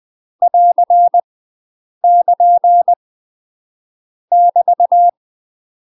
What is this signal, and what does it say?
Morse code, heard as a keyed tone, about 20 wpm, 700 Hz. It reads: <AR> <KN> <BT>
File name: UK MASTER QSOs TRAINING / 20WPM / UK MASTER QSOs 0026